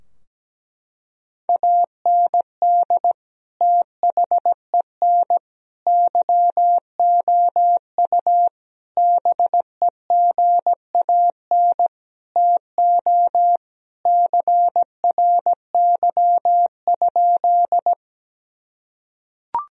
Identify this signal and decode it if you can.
Morse code, heard as a keyed tone, about 17 wpm, 700 Hz. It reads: AND THEN YOU BEGAN TO CRY?
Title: Sentences from Top 1000 Words - Review Part 2 - 17wpm